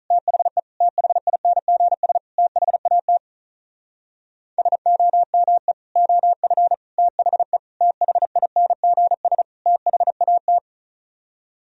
Morse code, heard as a keyed tone, about 47 wpm, 700 Hz. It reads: THE THINGS THAT SOME OF THE THINGS THAT